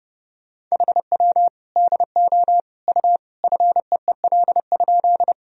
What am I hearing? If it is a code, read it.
Morse code, heard as a keyed tone, about 30 wpm, 700 Hz. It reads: HW DO U FEEL?
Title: Sentences from Top 400 Words and Abbreviations 30wpm